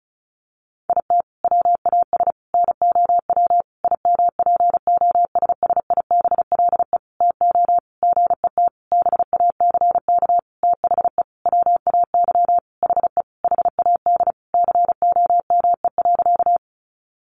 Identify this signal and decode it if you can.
Morse code, heard as a keyed tone, about 34 wpm, 700 Hz. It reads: IT WAS NOW IMPOSSIBLE TO GET BACK THE WAY HE HAD COME.
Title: Sentences from Top 1100 Words 35wpm